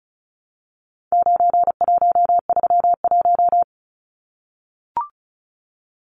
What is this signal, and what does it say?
Morse code, heard as a keyed tone, about 34 wpm, 700 Hz. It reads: 9131
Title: Numbers - 4-Digit 35wpm